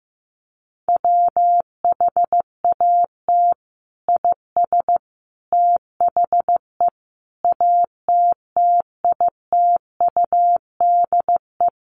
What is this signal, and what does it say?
Morse code, heard as a keyed tone, about 15 wpm, 700 Hz. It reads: WHAT IS THE ATTITUDE
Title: Sentences from Top 1300 Words 15wpm